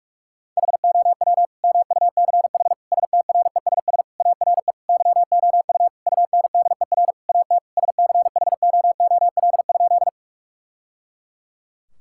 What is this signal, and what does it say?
Morse code, heard as a keyed tone, about 41 wpm, 700 Hz. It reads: HOW MUCH STRESS ARE YOU UNDER AT SCHOOL?
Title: Sentences from Top 1000 Words - Review Part 6 - 45wpm